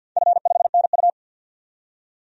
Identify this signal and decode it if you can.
Morse code, heard as a keyed tone, about 51 wpm, 700 Hz. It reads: F5NU
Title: DXCC-100 Call Signs - 4 Characters - 50wpm